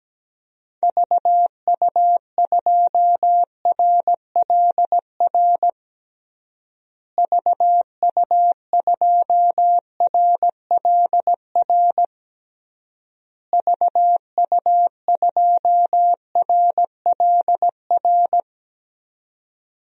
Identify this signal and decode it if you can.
Morse code, heard as a keyed tone, about 17 wpm, 700 Hz. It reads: VU2RLR VU2RLR VU2RLR
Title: DXCC-100 Call Signs - 6 Characters - Repeated 3x - 17wpm